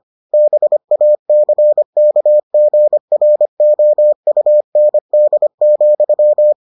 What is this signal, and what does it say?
Morse code, heard as a keyed tone, about 25 wpm, 600 Hz. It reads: BACKGROUND,